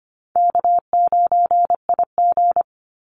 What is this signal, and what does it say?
Morse code, heard as a keyed tone, about 25 wpm, 700 Hz. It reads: K9IG
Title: Easy Call Signs Repeated 3x - 25wpm